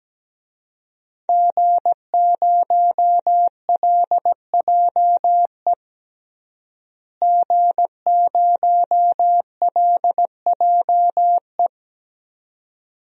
Morse code, heard as a keyed tone, about 17 wpm, 700 Hz. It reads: G0LJE G0LJE
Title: DXCC-100 Call Signs - 5 Characters - Repeated 2x - 17wpm